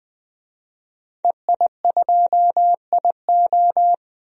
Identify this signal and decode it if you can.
Morse code, heard as a keyed tone, about 20 wpm, 700 Hz. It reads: EI2IO
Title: DXCC-100 Call Signs - 5 Characters - 20wpm